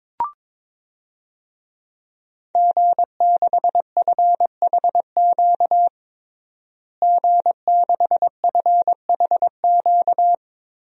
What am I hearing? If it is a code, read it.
Morse code, heard as a keyed tone, about 22 wpm, 700 Hz. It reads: G6FHQ G6FHQ
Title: DXCC-100 Call Signs - 5 Characters - Repeated 3x - 22wpm